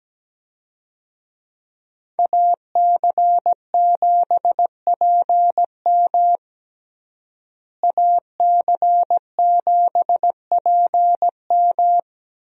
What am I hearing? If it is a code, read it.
Morse code, heard as a keyed tone, about 17 wpm, 700 Hz. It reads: AC7PM AC7PM
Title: USA Call Signs - 5 Characters - Repeated 2x - 17wpm